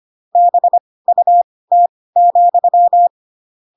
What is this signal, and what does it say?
Morse code, heard as a keyed tone, about 25 wpm, spaced as for 18 wpm, 700 Hz. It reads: BUT,
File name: Pinocchio / Pinocchio ch7 12 25 ik1hge